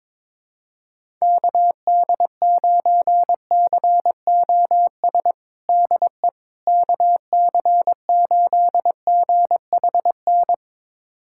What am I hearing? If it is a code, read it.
Morse code, heard as a keyed tone, about 22 wpm, 700 Hz. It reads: KD9COS DE KC8GHN
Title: USA Callsign DE Callsign - 6 Characters - 22wpm